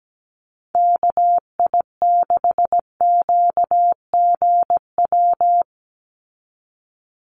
Morse code, heard as a keyed tone, about 17 wpm, 700 Hz. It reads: KI6QGW
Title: USA Call Signs 17wpm